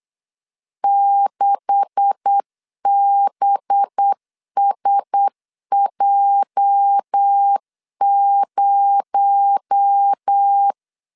Morse code, heard as a keyed tone, about 8 wpm, 800 Hz. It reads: 6BSJ0
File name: LEZIONI / 9 286 8wpm 3425